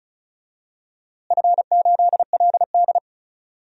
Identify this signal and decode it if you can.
Morse code, heard as a keyed tone, about 34 wpm, 700 Hz. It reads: F8LD
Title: DXCC-100 Call Signs - 4 Characters - Repeated 3x - 35wpm